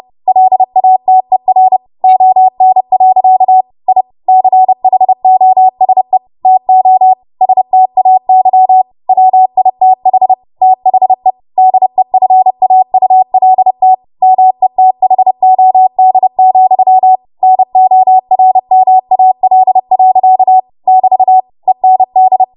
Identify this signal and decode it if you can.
Morse code, heard as a keyed tone, about 30 wpm, 750 Hz. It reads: LATER ON. I CHOSE TO STAY WITH THE DEFAULT METHOD, NORMAL. <BT> END
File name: Morse / 30WPM